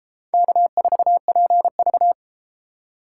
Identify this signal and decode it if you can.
Morse code, heard as a keyed tone, about 30 wpm, 700 Hz. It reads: K4PV